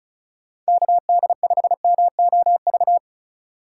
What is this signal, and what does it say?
Morse code, heard as a keyed tone, about 36 wpm, 700 Hz. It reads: KD5MOV
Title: USA Call Signs - 6 Characters - 3x Character Spacing - 35wpm